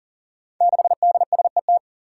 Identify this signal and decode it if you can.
Morse code, heard as a keyed tone, about 40 wpm, 690 Hz. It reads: 6DSET
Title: Five Challenging Letter-Number Combinations 40wpm